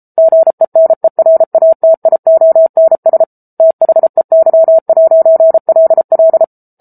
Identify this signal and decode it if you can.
Morse code, heard as a keyed tone, about 34 wpm, 650 Hz. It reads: GENERATIONS THEY'LL